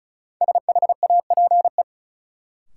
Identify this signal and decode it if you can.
Morse code, heard as a keyed tone, about 35 wpm, 700 Hz. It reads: SHAPE
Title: Sets of 5 Words Encore - New Word Drill - 35wpm